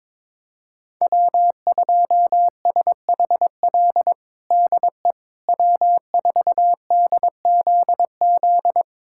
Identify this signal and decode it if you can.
Morse code, heard as a keyed tone, about 22 wpm, 700 Hz. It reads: W2SHL DE W4DZZ
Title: USA Callsign DE Callsign - 5 Characters - 22wpm